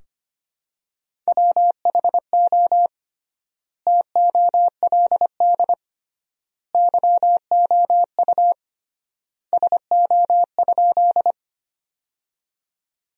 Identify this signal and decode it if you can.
Morse code, heard as a keyed tone, about 25 wpm, 700 Hz. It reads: WHO TOLD YOU SO?
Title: 3x Word-Spacing - Sentences from Top 400 Words - 25wpm